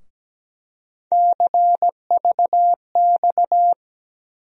Morse code, heard as a keyed tone, about 17 wpm, 700 Hz. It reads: CVX
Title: USA Call Sign Postfixes - 3 Letters 17wpm